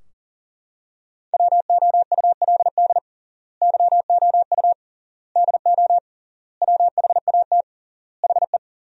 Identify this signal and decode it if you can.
Morse code, heard as a keyed tone, about 40 wpm, 700 Hz. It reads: WOULD YOU DO WHAT HE
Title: 3x Word-Spacing - Sentences from Top 100 Words - 40wpm